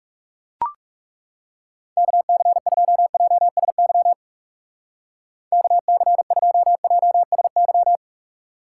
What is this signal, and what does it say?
Morse code, heard as a keyed tone, about 39 wpm, 700 Hz. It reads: KC2JSY KC2JSY